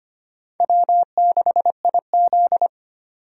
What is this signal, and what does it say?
Morse code, heard as a keyed tone, about 25 wpm, 700 Hz. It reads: W6IZ
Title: USA Call Signs - 4 Characters - 25wpm